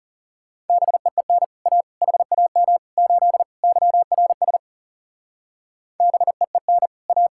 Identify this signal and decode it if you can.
Morse code, heard as a keyed tone, about 38 wpm, 700 Hz. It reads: BEEN A HAM 8 YRS BEEN A